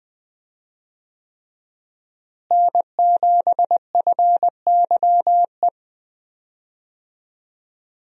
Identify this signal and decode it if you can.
Morse code, heard as a keyed tone, about 20 wpm, 700 Hz. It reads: N7FYE
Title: USA Call Signs - 5 Characters - Repeated 3x - 20wpm